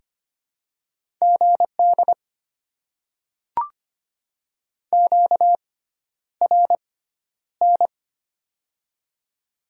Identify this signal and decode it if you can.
Morse code, heard as a keyed tone, about 25 wpm, spaced as for 11 wpm, 700 Hz. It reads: GD QRN
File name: Top 90 QSO Elements - 6x Character Spacing - 25wpm